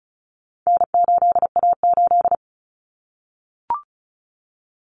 Morse code, heard as a keyed tone, about 34 wpm, 700 Hz. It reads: N8A8
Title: Common Cut Numbers - 4-Digit 35wpm